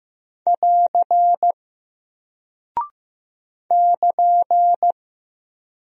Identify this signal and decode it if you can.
Morse code, heard as a keyed tone, about 15 wpm, 700 Hz. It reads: <AR> <KN>
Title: Single Character (Letter:Number:Punctuation:Prosign) Alt 15wpm